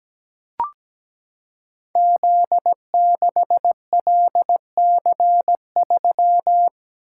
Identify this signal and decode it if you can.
Morse code, heard as a keyed tone, about 17 wpm, 700 Hz. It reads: Z6LC3
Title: Code Groups - 5 Letter:Numbers - 17wpm